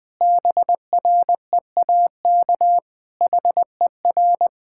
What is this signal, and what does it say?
Morse code, heard as a keyed tone, about 20 wpm, 700 Hz. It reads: BREAK HER
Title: Sentences from Top 1000 Words - Review Part 6 - 20wpm